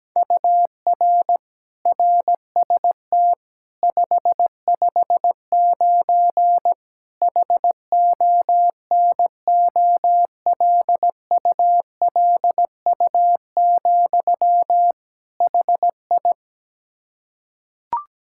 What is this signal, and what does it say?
Morse code, heard as a keyed tone, about 17 wpm, 700 Hz. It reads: UR RST 559 HONOLULU, HI
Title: Practice QSO - Exchange 1 - Beginning - 17wpm